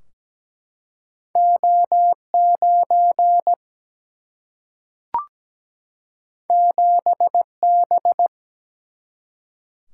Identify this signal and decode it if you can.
Morse code, heard as a keyed tone, about 17 wpm, 700 Hz. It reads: O9 7B
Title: Single-Letter Single-Number 17wpm